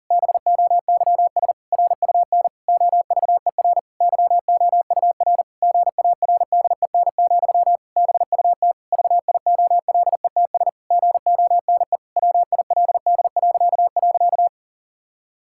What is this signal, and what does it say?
Morse code, heard as a keyed tone, about 40 wpm, 700 Hz. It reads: BOYS RUN OVER YOUR GARDEN, BUT VIOLETS GONE WILD..
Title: Pangram Challenge v2 - 40wpm